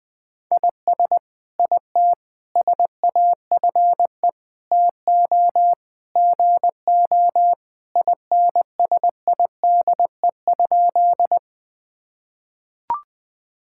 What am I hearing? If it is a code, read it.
Morse code, heard as a keyed tone, about 20 wpm, 700 Hz. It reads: IS IT SAFE TO GO INSIDE?
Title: Sentences from Top 1000 Words - Review Part 6 - 20wpm